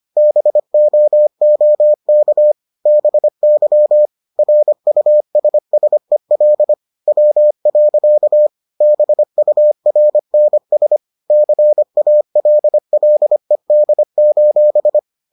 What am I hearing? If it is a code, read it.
Morse code, heard as a keyed tone, about 25 wpm, 600 Hz. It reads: BOOK BY RUSSEL W. BURNS CALLED: